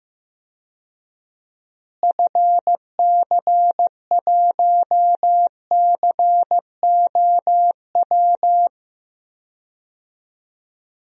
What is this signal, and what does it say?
Morse code, heard as a keyed tone, about 14 wpm, 700 Hz. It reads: FC1COW
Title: DXCC-100 Call Signs - 6 Characters - Repeated 3x - 15wpm